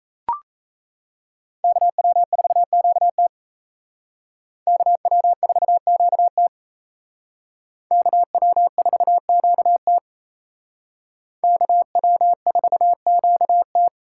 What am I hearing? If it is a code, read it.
Morse code, heard as a keyed tone, about 35 wpm, 700 Hz. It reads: KW4QT KW4QT KW4QT KW4QT